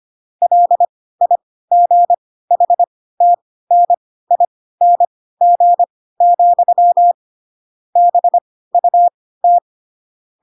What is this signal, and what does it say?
Morse code, heard as a keyed tone, about 25 wpm, spaced as for 16 wpm, 700 Hz. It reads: LIGHTNING, BUT